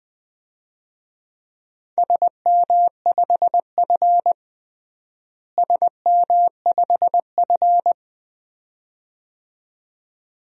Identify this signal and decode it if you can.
Morse code, heard as a keyed tone, about 21 wpm, 700 Hz. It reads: SM5F SM5F
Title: DXCC-100 Call Signs - 4 Characters - Repeated 2x - 20wpm